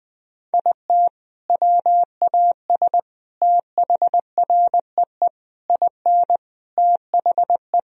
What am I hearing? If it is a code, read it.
Morse code, heard as a keyed tone, about 20 wpm, 700 Hz. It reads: IT WAS THREE IN THE
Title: Sentences from Top 1000 Words - Review Part 1 - 20wpm